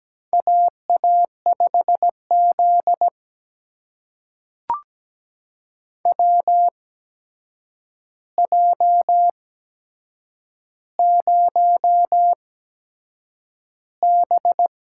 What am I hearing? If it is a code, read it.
Morse code, heard as a keyed tone, about 17 wpm, spaced as for 7 wpm, 700 Hz. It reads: AA5Z WJ0B
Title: USA Call Signs - 4 Characters - 8x Character Spacing - 17wpm